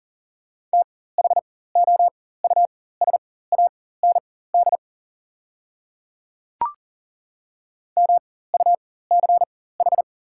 Morse code, heard as a keyed tone, about 40 wpm, spaced as for 19 wpm, 700 Hz. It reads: THOUSAND MUCH
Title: 4x Character Spacing - Top 500 Words - 40wpm